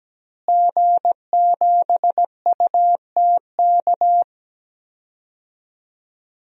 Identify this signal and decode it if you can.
Morse code, heard as a keyed tone, about 17 wpm, 700 Hz. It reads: G7UTK
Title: DXCC-100 Call Signs - 5 Characters - Repeated 2x - 17wpm